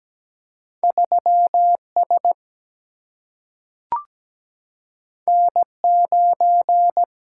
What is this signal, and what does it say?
Morse code, heard as a keyed tone, about 17 wpm, 700 Hz. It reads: 3S N9
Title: Single-Letter Single-Number 17wpm